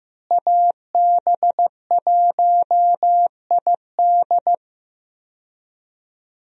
Morse code, heard as a keyed tone, about 15 wpm, 700 Hz. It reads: AB1ID
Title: USA Call Signs - 5 Characters - Repeated 3x - 15wpm